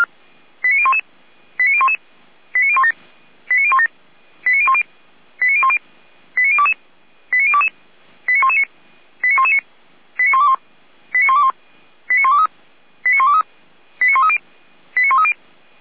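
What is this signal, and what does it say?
Five-tone selective call (ZVEI), heard as quick runs of five tones, tones of about 70 ms. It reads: 89011 89011 89017 89017 89010 89010 89022 89022 89119 89119 89121 89121 89123 89123 89130 89130
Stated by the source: ZVEI five-tone call received on an SDR, from Feuerwehr Orlamünde fire brigade.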